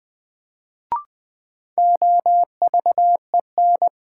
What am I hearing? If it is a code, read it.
Morse code, heard as a keyed tone, about 21 wpm, 700 Hz. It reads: OVEN
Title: Sentences from Top 1200 Words - New Word Drill Alt - 20wpm